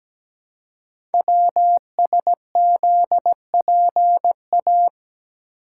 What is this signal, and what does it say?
Morse code, heard as a keyed tone, about 17 wpm, 700 Hz. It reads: WSZPA
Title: Code Groups - 5 Letters - 17wpm